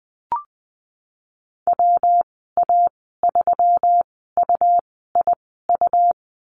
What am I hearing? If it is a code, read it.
Morse code, heard as a keyed tone, about 19 wpm, spaced as for 14 wpm, 700 Hz. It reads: WA3UIU
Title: 154 Lesson 25g - Letter p - Call Signs 20wpm